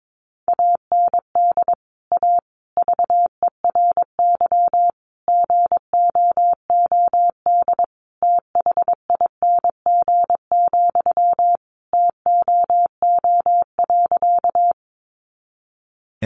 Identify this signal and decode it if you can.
Morse code, heard as a keyed tone, about 22 wpm, 700 Hz. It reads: AND A VERY GOOD THING, TOO.
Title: Sentences from Top 200 Words 22wpm